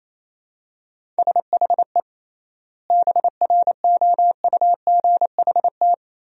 Morse code, heard as a keyed tone, about 28 wpm, 700 Hz. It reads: SHE BROUGHT
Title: 3x Word-Spacing - Sentences from Top 500 Words - 28wpm